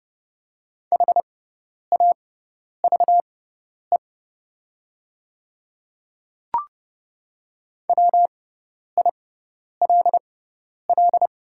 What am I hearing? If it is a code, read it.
Morse code, heard as a keyed tone, about 30 wpm, spaced as for 10 wpm, 690 Hz. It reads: HAVE WILL
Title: 6x Character Spacing - Top 200 Words - 30wpm